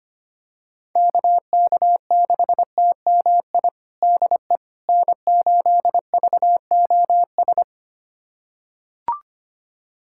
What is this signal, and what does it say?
Morse code, heard as a keyed tone, about 25 wpm, 700 Hz. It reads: KK6TMI DE N8VOS
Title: USA Callsign DE Callsign 25wpm